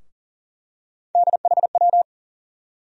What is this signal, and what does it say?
Morse code, heard as a keyed tone, about 40 wpm, 700 Hz. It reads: DHW
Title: USA Call Sign Postfixes - 3 Letters 40wpm